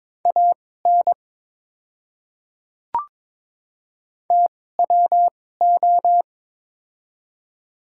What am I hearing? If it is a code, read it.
Morse code, heard as a keyed tone, about 22 wpm, spaced as for 16 wpm, 700 Hz. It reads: AN TWO
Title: 2x Character Spacing - Top 100 Words - 22wpm